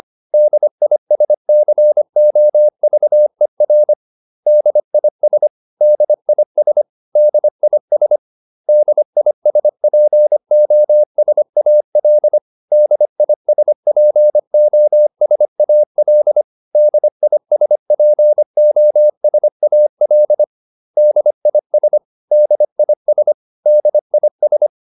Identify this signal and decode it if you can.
Morse code, heard as a keyed tone, about 25 wpm, 600 Hz. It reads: DISCOVER DIS DIS DIS DISPOSAL DISPOSAL DISPOSAL DIS DIS DIS
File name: DIS8 25